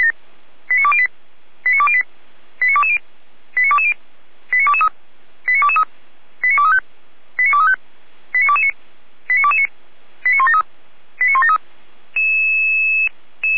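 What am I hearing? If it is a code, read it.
Five-tone selective call (ZVEI), heard as quick runs of five tones, tones of about 70 ms. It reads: 89208 89208 89220 89220 89223 89223 89235 89235 89229 89229 89173 89173
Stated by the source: ZVEI five-tone call received by SDR.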